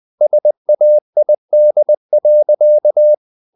Morse code, heard as a keyed tone, about 20 wpm, 600 Hz. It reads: SAID.
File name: the old man and the sea part 2 20wpm